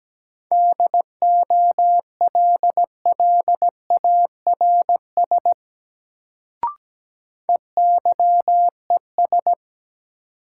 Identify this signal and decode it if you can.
Morse code, heard as a keyed tone, about 17 wpm, 700 Hz. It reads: DOLLARS EYES
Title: Sets of 5 Words - New Word Drill - 17wpm